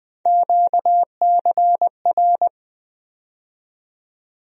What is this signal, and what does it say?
Morse code, heard as a keyed tone, about 20 wpm, 700 Hz. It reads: QCR
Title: USA Call Sign Postfixes - 3 Letters 20wpm